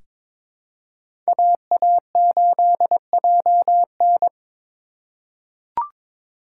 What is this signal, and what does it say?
Morse code, heard as a keyed tone, about 22 wpm, 700 Hz. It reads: AA8JN
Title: USA Call Signs - 2x Character Spacing - 22wpm